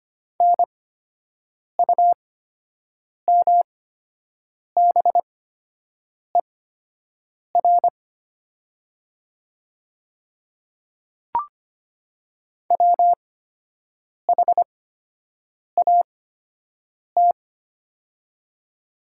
Morse code, heard as a keyed tone, about 25 wpm, spaced as for 7 wpm, 700 Hz. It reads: NUMBER WHAT
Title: 8x Character Spacing - Top 100 Words - 25wpm